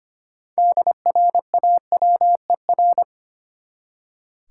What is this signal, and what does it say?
Morse code, heard as a keyed tone, about 25 wpm, 700 Hz. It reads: DRAWER
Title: Six Letter Words - 25wpm